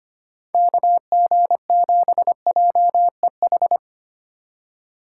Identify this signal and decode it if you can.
Morse code, heard as a keyed tone, about 25 wpm, 690 Hz. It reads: KG7JEH
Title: USA Call Signs - 6 Characters - 2x Character Spacing - 25wpm